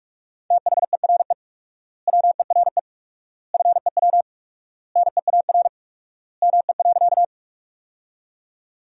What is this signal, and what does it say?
Morse code, heard as a keyed tone, about 46 wpm, 700 Hz. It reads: THERE WERE FEW NEAR ME.
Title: 4x Word-Spacing - Sentences from Top 300 Words - 45wpm